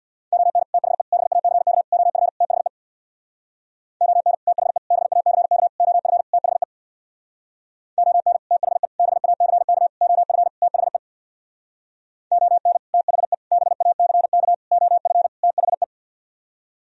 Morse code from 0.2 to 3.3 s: ON THE BACK OF THE

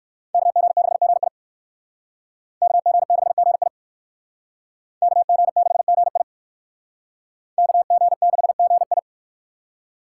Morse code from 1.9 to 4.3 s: KG6GI

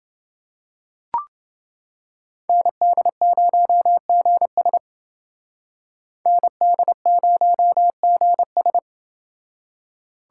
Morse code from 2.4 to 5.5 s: ND0GS

